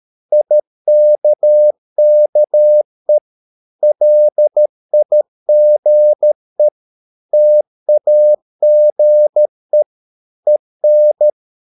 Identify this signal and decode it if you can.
Morse code, heard as a keyed tone, about 13 wpm, 600 Hz. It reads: IKKE LIGE TAGE EN
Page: Ferie 13 wpm